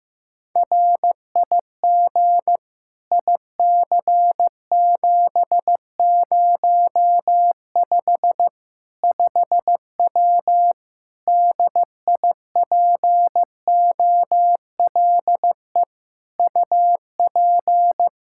Morse code, heard as a keyed tone, about 15 wpm, 700 Hz. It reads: RIG IC705 5W DIPOLE UP
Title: K1USN SST QSO Practice 15wpm